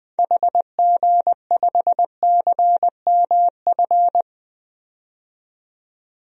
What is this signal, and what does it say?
Morse code, heard as a keyed tone, about 21 wpm, 700 Hz. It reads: HG5CMF
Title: DXCC-100 Call Signs - 6 Characters - 20wpm